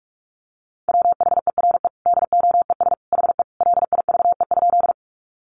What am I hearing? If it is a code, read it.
Morse code, heard as a keyed tone, about 44 wpm, 700 Hz. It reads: WHERE DOES HE LIVE?
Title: Sentences from Top 200 Words 45wpm